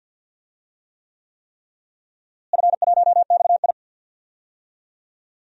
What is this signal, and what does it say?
Morse code, heard as a keyed tone, about 53 wpm, 700 Hz. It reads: F1XI
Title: DXCC-100 Call Signs - 4 Characters - Repeated 3x - 50wpm